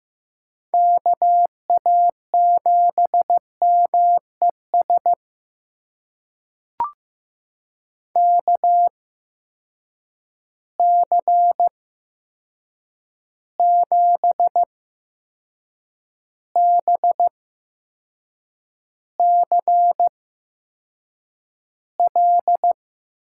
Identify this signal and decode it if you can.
Morse code, heard as a keyed tone, about 15 wpm, spaced as for 6 wpm, 700 Hz. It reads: KA7MES KC7BCL
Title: USA Call Signs - 6 Characters - 8x Character Spacing - 15wpm